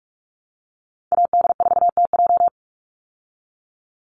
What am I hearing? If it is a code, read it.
Morse code, heard as a keyed tone, about 44 wpm, 700 Hz. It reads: AD4TJ
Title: Call Signs Repeated 3x - 45wpm